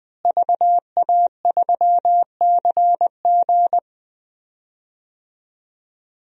Morse code, heard as a keyed tone, about 20 wpm, 700 Hz. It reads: VA3CG
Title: DXCC-100 Call Signs - 5 Characters - Repeated 2x - 20wpm